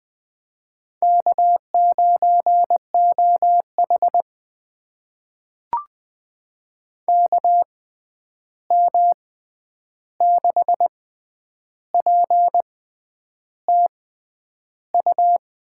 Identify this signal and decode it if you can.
Morse code, heard as a keyed tone, about 21 wpm, spaced as for 9 wpm, 700 Hz. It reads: K9OH KM6PTU
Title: USA Call Signs - 6x Character Spacing - 20wpm